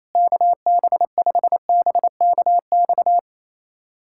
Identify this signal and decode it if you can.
Morse code, heard as a keyed tone, about 28 wpm, 700 Hz. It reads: KB5BKX
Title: USA Callsign DE Callsign 28wpm